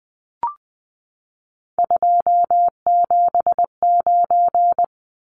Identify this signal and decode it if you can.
Morse code, heard as a keyed tone, about 20 wpm, 700 Hz. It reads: 279
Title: Numbers - 3-Digit 20wpm